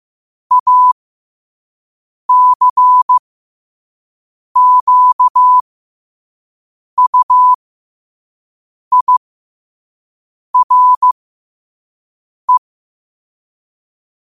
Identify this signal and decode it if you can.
Morse code, heard as a keyed tone, about 15 wpm, spaced as for 5 wpm, 1000 Hz. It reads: ACQUIRE
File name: qotdfiles / 5wpm